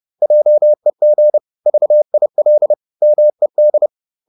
Morse code, heard as a keyed tone, about 30 wpm, 600 Hz. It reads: JEG VIL MED
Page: Ferie 30 wpm